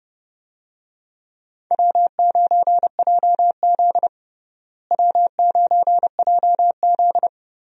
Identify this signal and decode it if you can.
Morse code, heard as a keyed tone, about 30 wpm, 700 Hz. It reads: W9JZ W9JZ
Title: USA Call Signs - 4 Characters - Repeated 2x - 30wpm